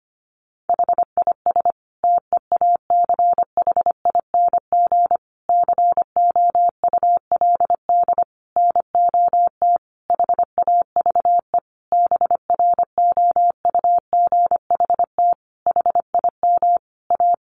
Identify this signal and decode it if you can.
Morse code, heard as a keyed tone, about 25 wpm, 700 Hz. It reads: HIS TEACHING COULD NOT HAVE BROUGHT HIM A
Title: Sentences from Top 1100 Words 25wpm